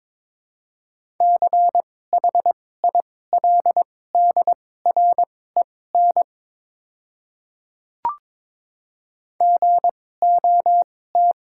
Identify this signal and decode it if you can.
Morse code, heard as a keyed tone, about 22 wpm, spaced as for 16 wpm, 700 Hz. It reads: CHILDREN GOT
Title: 2x Character Spacing - Top 300 Words - 22wpm